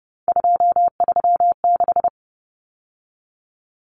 Morse code, heard as a keyed tone, about 30 wpm, 700 Hz. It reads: 236
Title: Numbers - 3-Digit 30wpm